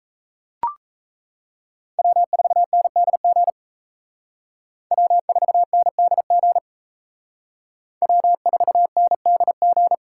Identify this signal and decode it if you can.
Morse code, heard as a keyed tone, about 32 wpm, 700 Hz. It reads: W4NDG W4NDG W4NDG
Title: Speed Racing - USA 1x3 Call Signs 28wpm